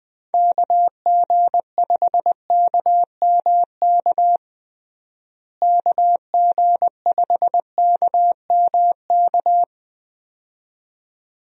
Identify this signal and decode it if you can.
Morse code, heard as a keyed tone, about 20 wpm, 700 Hz. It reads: KG5KMK KG5KMK
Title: USA Call Signs - 6 Characters - Repeated 2x - 20wpm